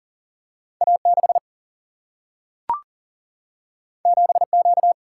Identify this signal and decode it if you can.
Morse code, heard as a keyed tone, about 39 wpm, 700 Hz. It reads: A6 7Q
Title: Single-Letter Single-Number 40wpm